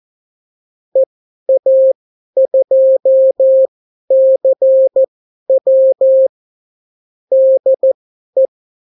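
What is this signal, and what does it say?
Morse code, heard as a keyed tone, about 14 wpm, spaced as for 11 wpm, 530 Hz. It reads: EA2CW DE
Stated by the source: Example Spanish CW QSO from a Morse learning book, 14 wpm characters with 8 wpm spacing.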